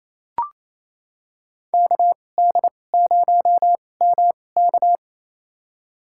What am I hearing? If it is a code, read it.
Morse code, heard as a keyed tone, about 27 wpm, spaced as for 20 wpm, 700 Hz. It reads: KD0MK
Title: USA Call Signs - 2x Character Spacing - 28wpm